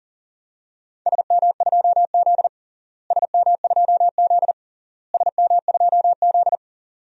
Morse code, heard as a keyed tone, about 40 wpm, 700 Hz. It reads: SM2Z SM2Z SM2Z
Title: DXCC-100 Call Signs - 4 Characters - Repeated 3x - 40wpm